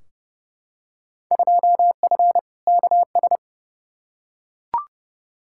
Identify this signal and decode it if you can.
Morse code, heard as a keyed tone, about 30 wpm, 700 Hz. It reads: 2F KS
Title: Field Day Exchanges - 30wpm